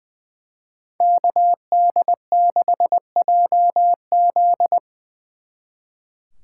Morse code, heard as a keyed tone, about 24 wpm, 700 Hz. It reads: KD6JZ
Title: Speed Racing - USA 2x2 Call Signs 20wpm